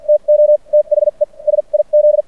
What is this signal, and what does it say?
Morse code, heard as a keyed tone, about 50 wpm, spaced as for 39 wpm, 590 Hz. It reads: TO THE SIZ
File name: archived / CRSnet